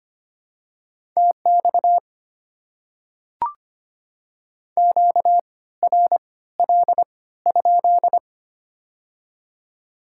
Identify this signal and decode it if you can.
Morse code, heard as a keyed tone, about 25 wpm, spaced as for 16 wpm, 700 Hz. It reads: TX QRL?
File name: Top 90 QSO Elements - 3x Character Spacing - 25wpm